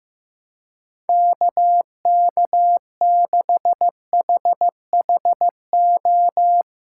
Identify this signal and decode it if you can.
Morse code, heard as a keyed tone, about 15 wpm, 700 Hz. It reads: KK6HHO
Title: USA Callsign DE Callsign 15wpm